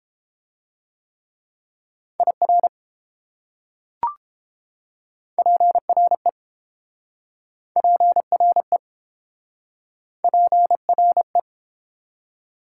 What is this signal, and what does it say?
Morse code, heard as a keyed tone, about 29 wpm, 700 Hz. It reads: IR PRE PRE PRE